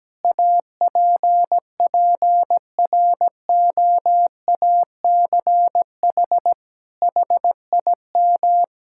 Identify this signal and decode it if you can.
Morse code, heard as a keyed tone, about 17 wpm, 700 Hz. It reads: APPROACH HIM
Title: Sentences from Top 1000 Words - Review Part 6 - 17wpm